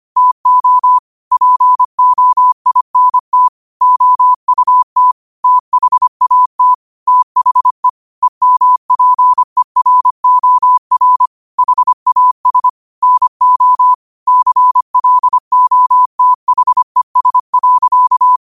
Morse code, heard as a keyed tone, about 25 wpm, 1000 Hz. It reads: TO POINT OUT THAT THE EMPEROR HAS NO CLOTHES.
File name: qotdfiles / 25wpm